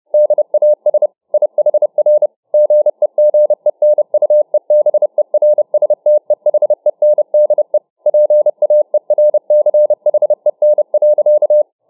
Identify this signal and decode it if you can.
Morse code, heard as a keyed tone, about 29 wpm, 600 Hz. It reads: DAS IHR GEGENUEBERSTEHENDE PAERCHEN.